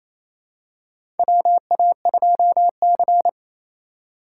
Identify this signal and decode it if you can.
Morse code, heard as a keyed tone, about 28 wpm, 700 Hz. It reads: WA2C
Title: USA Call Signs - 4 Characters - 2x Character Spacing - 28wpm